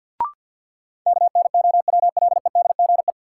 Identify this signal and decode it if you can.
Morse code, heard as a keyed tone, about 51 wpm, 700 Hz. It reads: KNOWLEDGE
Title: Sentences from Top 600 Words - New Word Drill - 50wpm